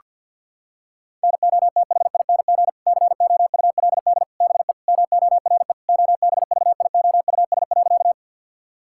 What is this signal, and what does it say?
Morse code, heard as a keyed tone, about 50 wpm, 700 Hz. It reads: NOTHING COULD BE MORE OBVIOUS.
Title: Sentences from Top 1600 Words 50wpm